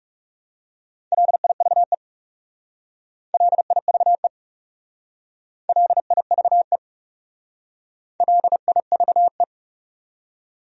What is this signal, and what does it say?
Morse code, heard as a keyed tone, about 37 wpm, 700 Hz. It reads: LIVE LIVE LIVE LIVE
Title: Speed-Racing Top 300 Words 30wpm